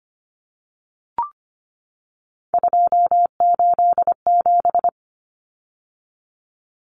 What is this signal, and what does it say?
Morse code, heard as a keyed tone, about 25 wpm, 700 Hz. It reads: 287
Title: Numbers - 3-Digit 25wpm